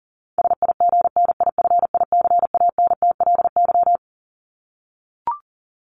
Morse code, heard as a keyed tone, about 40 wpm, 700 Hz. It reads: SIGNIFICANTLY
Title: LY Suffix - Top 100 Words with Base Words 40wpm